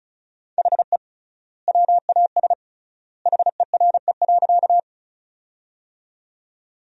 Morse code, heard as a keyed tone, about 34 wpm, 700 Hz. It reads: HE WAS HERE.
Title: 3x Word-Spacing - Sentences from Top 200 Words - 35wpm